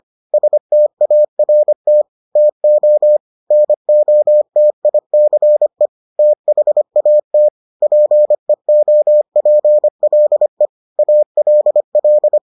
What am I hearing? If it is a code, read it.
Morse code, heard as a keyed tone, about 25 wpm, 600 Hz. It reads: START TO NOTICE THAT PEOPLE ALL